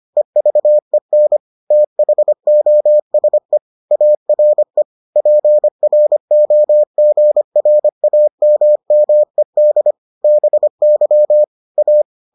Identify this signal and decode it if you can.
Morse code, heard as a keyed tone, about 25 wpm, 600 Hz. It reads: EVEN THOSE ARE PROGRAMMED BY A